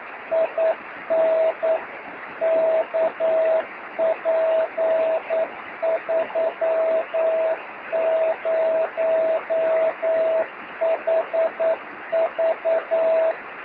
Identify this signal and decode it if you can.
Morse code, heard as a keyed tone, about 9 wpm, 650 Hz, with noise. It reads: IN KP30HV
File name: sm7lcb / oh5shf 13cm